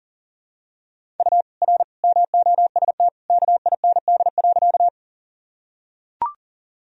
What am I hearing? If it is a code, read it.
Morse code, heard as a keyed tone, about 40 wpm, 700 Hz. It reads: U R MOST KIND.
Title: Sentences from Top 200 Words and Abbreviations 40wpm